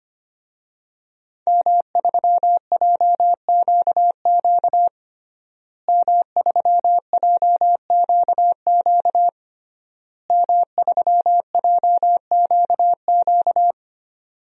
Morse code, heard as a keyed tone, about 25 wpm, 700 Hz. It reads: M3JQQ M3JQQ M3JQQ
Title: DXCC-100 Call Signs - 5 Characters - Repeated 3x - 25wpm